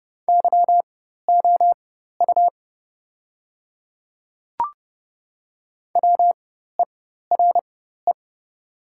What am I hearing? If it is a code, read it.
Morse code, heard as a keyed tone, about 30 wpm, spaced as for 14 wpm, 700 Hz. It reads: YOU WERE
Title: 4x Character Spacing - Top 100 Words - 30wpm